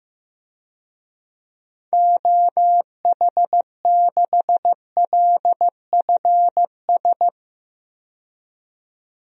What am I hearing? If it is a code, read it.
Morse code, heard as a keyed tone, about 15 wpm, 700 Hz. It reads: OH6LFS
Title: DXCC-100 Call Signs - 6 Characters - 15wpm